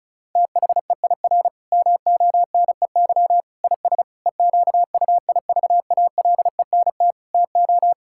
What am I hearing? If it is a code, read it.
Morse code, heard as a keyed tone, about 36 wpm, 690 Hz. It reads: THEIR MONEY IS EQUIVALENT TO
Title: Sentences from Top 1600 Words 35wpm